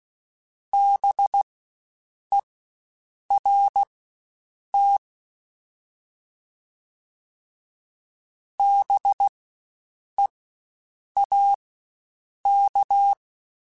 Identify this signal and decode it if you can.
Morse code, heard as a keyed tone, about 16 wpm, spaced as for 7 wpm, 800 Hz. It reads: BERT BEAK
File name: Spaced Alphas / Spaced Alphas mp3s / Bs sp @15